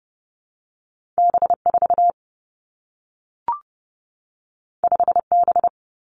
Morse code, heard as a keyed tone, about 30 wpm, 700 Hz. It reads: B 4 5 B
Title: Two Challenging Letter-Number Combinations 30wpm